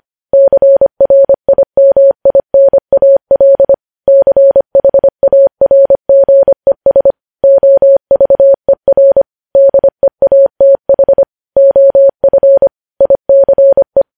Morse code, heard as a keyed tone, about 25 wpm, 550 Hz. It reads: CRIMINAL CHARGES OVER DEATH OF ICE
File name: podcast / News headlines-1x-25x25